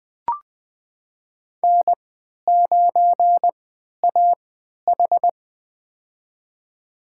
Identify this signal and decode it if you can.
Morse code, heard as a keyed tone, about 20 wpm, spaced as for 11 wpm, 700 Hz. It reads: N9AH
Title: USA Call Signs - 3x Character Spacing - 20wpm